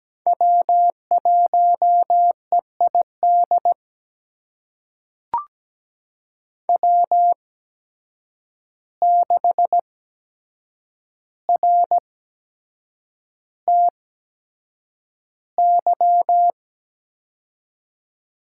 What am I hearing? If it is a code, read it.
Morse code, heard as a keyed tone, about 17 wpm, spaced as for 7 wpm, 700 Hz. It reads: W1EID W6RTY
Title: USA Call Signs - 5 Characters - 8x Character Spacing - 17wpm